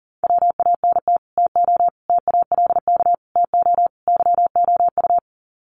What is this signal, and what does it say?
Morse code, heard as a keyed tone, about 40 wpm, 700 Hz. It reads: WANT TO TALK TO YOU
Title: Sets of 5 Words Encore 40wpm